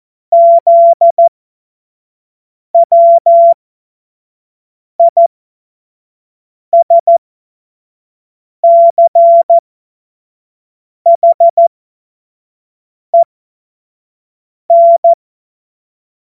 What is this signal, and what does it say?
Morse code, heard as a keyed tone, about 14 wpm, spaced as for 5 wpm, 680 Hz. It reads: ZWISCHEN
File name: Morsekurs / Witze / Witz 21. Dezember 2021, 50 F70